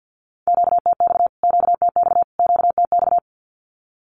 Morse code, heard as a keyed tone, about 50 wpm, 700 Hz. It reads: QTX QTX QTX
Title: Q-Codes Repeated 3x 50wpm